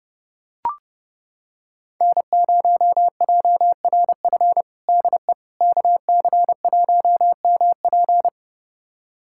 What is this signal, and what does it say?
Morse code, heard as a keyed tone, about 30 wpm, 700 Hz. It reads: N0JRF DE KC1MP
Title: USA Callsign DE Callsign - 5 Characters - 30wpm